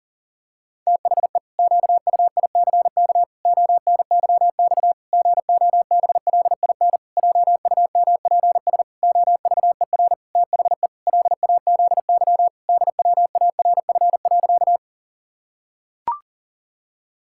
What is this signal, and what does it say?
Morse code, heard as a keyed tone, about 40 wpm, 700 Hz. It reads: THE QUICK ONYX GOBLIN JUMPS OVER THE LAZY DWARF.
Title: Pangram Challenge v2 - 40wpm